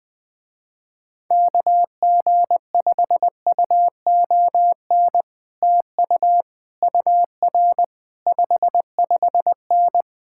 Morse code, heard as a keyed tone, about 20 wpm, 700 Hz. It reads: KG5UON TU UR 55N
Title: Speed-Racing Common QSO Phrases Encore - 17wpm